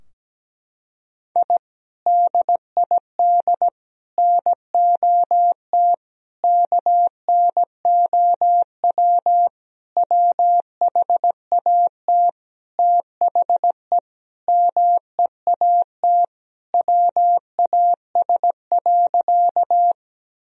Morse code, heard as a keyed tone, about 17 wpm, 700 Hz. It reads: I DID NOT KNOW WHAT THE MEAT WAS.
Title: Sentences from Top 1000 Words - Review Part 3 - 17wpm